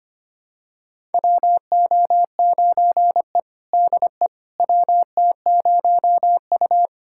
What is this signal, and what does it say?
Morse code, heard as a keyed tone, about 25 wpm, 700 Hz. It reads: WO9E DE WT0U
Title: USA Callsign DE Callsign - 4 Characters - 25wpm